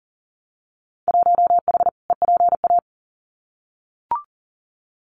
Morse code, heard as a keyed tone, about 40 wpm, 700 Hz. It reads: 1H EPA
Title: Winter Field Day Exchanges 40wpm